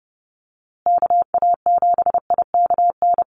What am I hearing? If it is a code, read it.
Morse code, heard as a keyed tone, about 30 wpm, 700 Hz. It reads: KA7IKN
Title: USA Call Signs Repeated 2x 30wpm